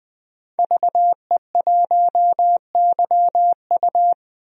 Morse code, heard as a keyed tone, about 20 wpm, 700 Hz. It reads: VE1YU
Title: DXCC-100 Call Signs - 5 Characters - 20wpm